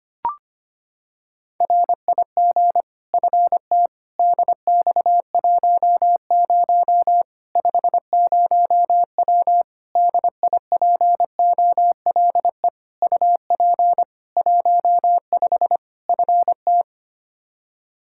Morse code, heard as a keyed tone, about 25 wpm, 700 Hz. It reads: RIG FT DX10 50W DIPOLE UP 15 FT
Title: K1USN SST QSO Practice 25wpm